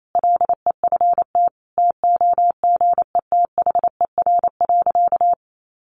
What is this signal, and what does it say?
Morse code, heard as a keyed tone, about 28 wpm, 700 Hz. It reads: LEFT TOGETHER.
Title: Sentences from Top 1100 Words 28wpm